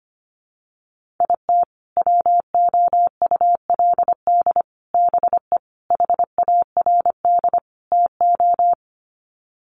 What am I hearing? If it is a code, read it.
Morse code, heard as a keyed tone, about 25 wpm, 700 Hz. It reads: IT WOULD BE HARD TO
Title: Sets of 5 Words Encore2 25wpm